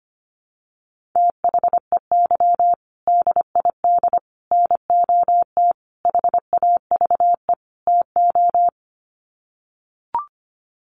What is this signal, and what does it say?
Morse code, heard as a keyed tone, about 25 wpm, 700 Hz. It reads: THEY DID NOT HAVE TO
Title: Sets of 5 Words Encore2 25wpm